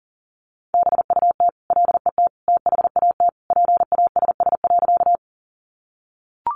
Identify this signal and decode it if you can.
Morse code, heard as a keyed tone, about 40 wpm, 700 Hz. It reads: BUT LET THAT PASS.
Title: Sentences from Top 400 Words 40wpm